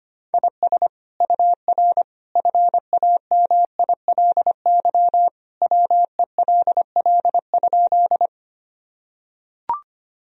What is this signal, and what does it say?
Morse code, heard as a keyed tone, about 25 wpm, 700 Hz. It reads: IS UR FAMILY WELL?
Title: Sentences from Top 400 Words and Abbreviations 25wpm